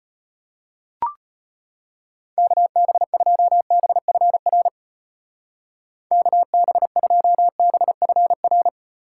Morse code, heard as a keyed tone, about 31 wpm, 700 Hz. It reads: KB2BFR KB2BFR
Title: Speed Racing - USA 2x3 Call Signs 25wpm